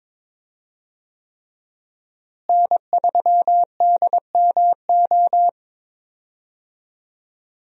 Morse code, heard as a keyed tone, about 22 wpm, 700 Hz. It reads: N3DMO
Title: USA Call Signs - 5 Characters - Repeated 3x - 22wpm